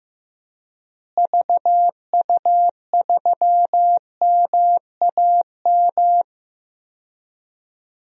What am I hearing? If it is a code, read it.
Morse code, heard as a keyed tone, about 15 wpm, 700 Hz. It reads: VU3MAM
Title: DXCC-100 Call Signs - 6 Characters - Repeated 2x - 15wpm